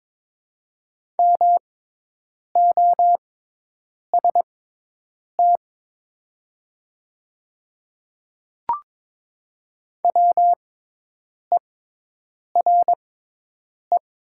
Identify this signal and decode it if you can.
Morse code, heard as a keyed tone, about 22 wpm, spaced as for 8 wpm, 700 Hz. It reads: MOST WERE
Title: 6x Character Spacing - Top 100 Words - 22wpm